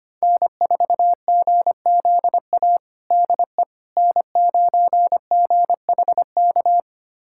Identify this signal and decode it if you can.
Morse code, heard as a keyed tone, about 25 wpm, 700 Hz. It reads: N4GZA DE N9GHK
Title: USA Callsign DE Callsign - 5 Characters - 25wpm